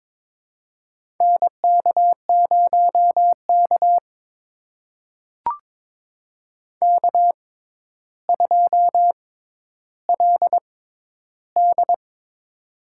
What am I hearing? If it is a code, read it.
Morse code, heard as a keyed tone, about 22 wpm, spaced as for 11 wpm, 700 Hz. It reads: NK0K K2LD
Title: USA Call Signs - 6x Character Spacing - 22wpm